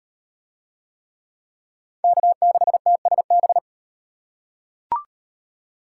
Morse code, heard as a keyed tone, about 34 wpm, 700 Hz. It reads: K6TSB